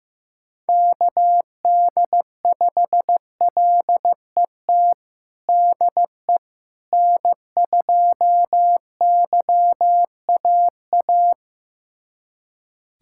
Morse code, heard as a keyed tone, about 15 wpm, 700 Hz. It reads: KD5LET DE N2YAA
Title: USA Callsign DE Callsign 15wpm